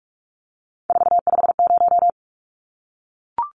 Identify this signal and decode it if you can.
Morse code, heard as a keyed tone, about 44 wpm, 700 Hz. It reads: 450
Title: Numbers - 3-Digit 45wpm